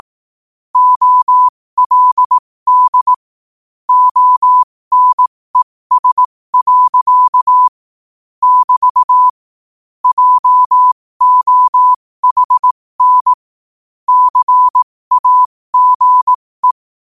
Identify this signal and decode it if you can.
Morse code, heard as a keyed tone, about 18 wpm, spaced as for 15 wpm, 1000 Hz. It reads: OLD ONES. <BT> JOHN CAGE
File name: qotdfiles / 15wpm